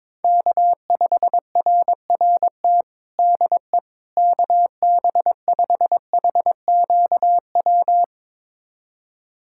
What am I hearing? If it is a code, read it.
Morse code, heard as a keyed tone, about 22 wpm, 700 Hz. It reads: K5RRT DE KB5HQW
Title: USA Callsign DE Callsign 22wpm